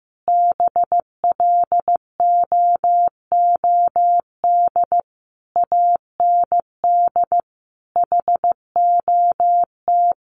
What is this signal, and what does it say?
Morse code, heard as a keyed tone, about 15 wpm, 700 Hz. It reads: BLOOD AND HOT
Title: Sentences from Top 1100 Words 15wpm